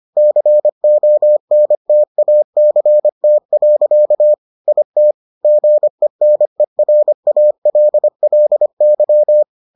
Morse code, heard as a keyed tone, about 25 wpm, 600 Hz. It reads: CONTACT. IT GENERALLY